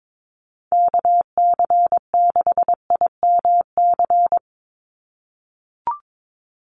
Morse code, heard as a keyed tone, about 22 wpm, 700 Hz. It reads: KC6IMC
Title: USA Call Signs 22wpm